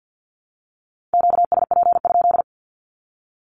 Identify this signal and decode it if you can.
Morse code, heard as a keyed tone, about 50 wpm, 700 Hz. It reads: QSP?
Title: Q-Codes Repeated 3x 50wpm